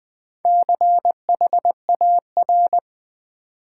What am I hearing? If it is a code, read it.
Morse code, heard as a keyed tone, about 20 wpm, 700 Hz. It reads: CHAR
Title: Build A Word - 4 Letters At A Time - 12-Letter Words - 20wpm